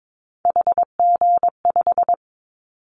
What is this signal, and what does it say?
Morse code, heard as a keyed tone, about 22 wpm, 700 Hz. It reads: HG5
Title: Callsign Prefixes for Top 25 DXCC - 22wpm